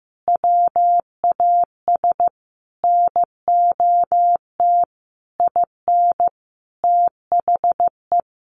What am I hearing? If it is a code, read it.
Morse code, heard as a keyed tone, about 15 wpm, 700 Hz. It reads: WAS NOT IN THE
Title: Sentences from Top 1500 Words 15wpm